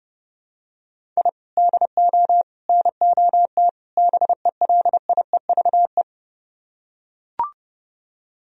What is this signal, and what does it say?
Morse code, heard as a keyed tone, about 32 wpm, 700 Hz. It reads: I DO NOT BELIEVE